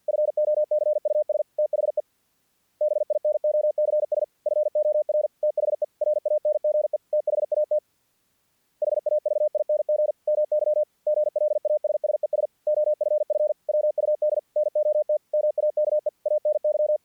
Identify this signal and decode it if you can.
Morse code, heard as a keyed tone, about 48 wpm, 600 Hz. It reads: FOCUS THE BINOCS FOR THE RANGE THAT HAVING MY GLASSES OFF WUD NOT MAKE ANY
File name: archived / CRSnet